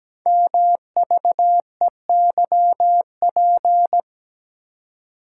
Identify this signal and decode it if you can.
Morse code, heard as a keyed tone, about 17 wpm, 700 Hz. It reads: MVEYP
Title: Code Groups - 5 Letters - 17wpm